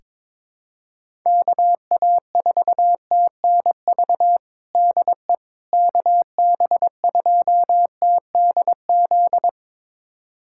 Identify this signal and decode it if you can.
Morse code, heard as a keyed tone, about 22 wpm, 700 Hz. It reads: KA4TNV DE KB2TDZ
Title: USA Callsign DE Callsign - 6 Characters - 22wpm